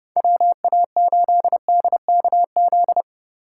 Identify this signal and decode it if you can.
Morse code, heard as a keyed tone, about 32 wpm, 700 Hz. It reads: WA8DKZ
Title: USA Call Signs - 6 Characters - 4x Character Spacing - 30wpm